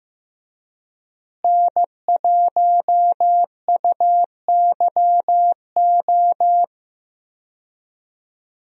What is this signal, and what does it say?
Morse code, heard as a keyed tone, about 15 wpm, 700 Hz. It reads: N1UYO
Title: USA Call Signs - 5 Characters - Repeated 3x - 15wpm